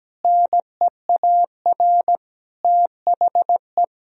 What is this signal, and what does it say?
Morse code, heard as a keyed tone, about 17 wpm, 700 Hz. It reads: NEAR THE
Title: Sentences from Top 1000 Words - Review Part 1 - 17wpm